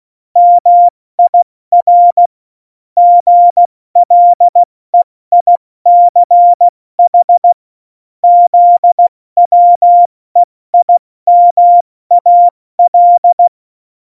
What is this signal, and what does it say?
Morse code, heard as a keyed tone, about 16 wpm, 700 Hz. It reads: MIR GLEICH ZWEIMAL
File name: Morsekurs / Witze / Witz 05. Dezember 2017, 70 F80